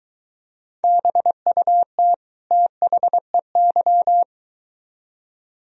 Morse code, heard as a keyed tone, about 23 wpm, 700 Hz. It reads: BUT THEY